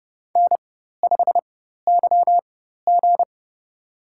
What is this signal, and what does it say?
Morse code, heard as a keyed tone, about 30 wpm, spaced as for 14 wpm, 700 Hz. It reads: N5YG
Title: USA Call Signs - 4 Characters - 4x Character Spacing - 30wpm